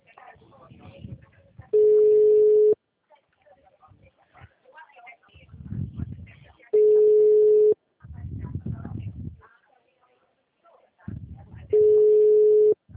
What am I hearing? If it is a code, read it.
Ringback tone (Europe, 425 Hz), with speech.